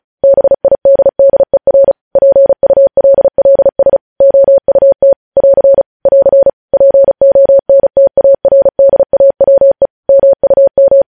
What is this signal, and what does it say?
Morse code, heard as a keyed tone, about 36 wpm, 550 Hz. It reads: BIDDER PULLS OUT <AR> <AR> PONTARDAWE MUM
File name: podcast / News headlines-1x-35x35